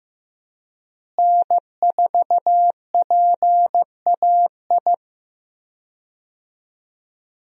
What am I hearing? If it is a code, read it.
Morse code, heard as a keyed tone, about 15 wpm, 700 Hz. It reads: N4PAI
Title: USA Call Signs - 5 Characters - Repeated 2x - 15wpm